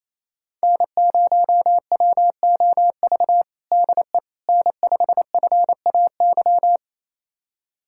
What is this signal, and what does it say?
Morse code, heard as a keyed tone, about 28 wpm, 700 Hz. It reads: N0WOV DE N5FAY
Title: USA Callsign DE Callsign - 5 Characters - 28wpm